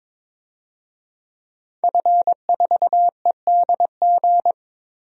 Morse code, heard as a keyed tone, about 22 wpm, 700 Hz. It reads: F4EDG
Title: DXCC-100 Call Signs - 5 Characters - 22wpm